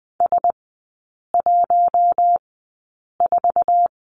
Morse code, heard as a keyed tone, about 20 wpm, spaced as for 8 wpm, 700 Hz. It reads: S14
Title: 047 Lesson 9e - Number 4 - Three Characters 20wpm